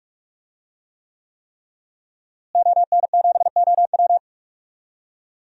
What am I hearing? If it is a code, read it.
Morse code, heard as a keyed tone, about 47 wpm, 700 Hz. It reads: ON7OW
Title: DXCC-100 Call Signs - 5 Characters - Repeated 3x - 45wpm